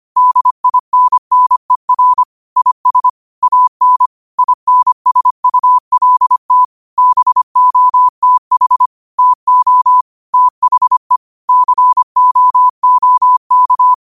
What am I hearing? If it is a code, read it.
Morse code, heard as a keyed tone, about 25 wpm, 1000 Hz. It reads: DINNER IS AN INSULT BOTH TO THE COOK
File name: 25wpm